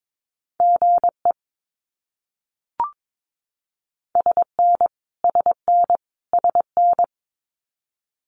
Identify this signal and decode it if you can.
Morse code, heard as a keyed tone, about 22 wpm, 700 Hz. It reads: GE SN SN SN
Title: Abbreviations Repeated 3x - 22wpm